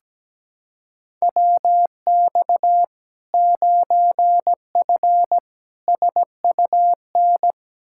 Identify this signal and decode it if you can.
Morse code, heard as a keyed tone, about 17 wpm, 700 Hz. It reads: WX 9F SUN
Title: Practice QSO - Exchange 2 - Info - 17wpm